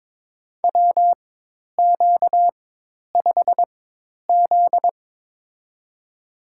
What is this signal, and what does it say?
Morse code, heard as a keyed tone, about 22 wpm, spaced as for 10 wpm, 700 Hz. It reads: WQ5Z
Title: USA Call Signs - 4 Characters - 4x Character Spacing - 22wpm